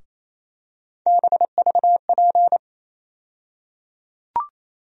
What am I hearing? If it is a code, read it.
Morse code, heard as a keyed tone, about 28 wpm, 700 Hz. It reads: BVP
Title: USA Call Sign Postfixes - 3 Letters 28wpm